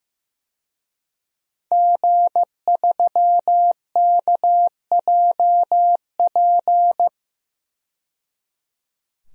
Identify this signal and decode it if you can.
Morse code, heard as a keyed tone, about 15 wpm, 700 Hz. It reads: G3KJP
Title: DXCC-100 Call Signs - 5 Characters - 15wpm